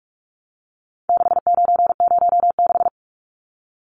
Morse code, heard as a keyed tone, about 44 wpm, 700 Hz. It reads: 6906
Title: Numbers - 4-Digit 45wpm